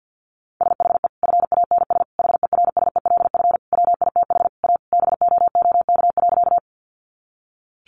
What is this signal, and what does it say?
Morse code, heard as a keyed tone, about 50 wpm, 700 Hz. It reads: SHE FANS HERSELF WITH A BOOK.
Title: Additional Easy Phrases 50wpm